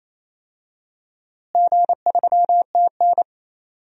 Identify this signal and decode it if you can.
Morse code, heard as a keyed tone, about 28 wpm, 700 Hz. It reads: G3TN
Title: DXCC-100 Call Signs - 4 Characters - Repeated 3x - 28wpm